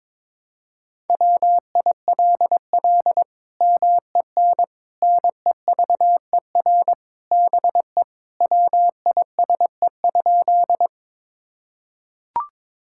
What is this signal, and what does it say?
Morse code, heard as a keyed tone, about 22 wpm, 700 Hz. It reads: WILL MEN NEVER BE WISE?
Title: Sentences from Top 1600 Words 22wpm